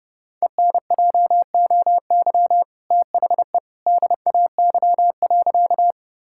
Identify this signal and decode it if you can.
Morse code, heard as a keyed tone, about 31 wpm, 700 Hz. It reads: ENJOY THE DAY.
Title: Sentences from Top 1000 Words - Review Part 5 - 30wpm